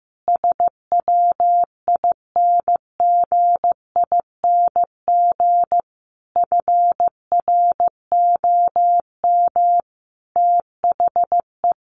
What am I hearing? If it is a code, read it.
Morse code, heard as a keyed tone, about 15 wpm, 700 Hz. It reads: SWINGING FROM THE
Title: Easy Phrases 15wpm